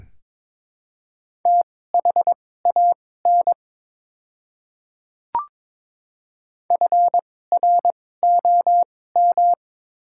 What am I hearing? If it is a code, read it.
Morse code, heard as a keyed tone, about 22 wpm, spaced as for 16 wpm, 690 Hz. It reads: THAN FROM
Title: 2x Character Spacing - Top 100 Words - 22wpm